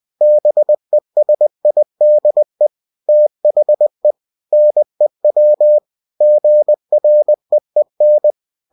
Morse code, heard as a keyed tone, about 20 wpm, 600 Hz. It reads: BESIDE THE NEW GREEN
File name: cw-books / winnie-the-pooh / winnie-20wpm-page-0028